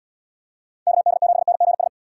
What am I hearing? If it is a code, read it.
Morse code, heard as a keyed tone, about 70 wpm, 700 Hz. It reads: KD7NGS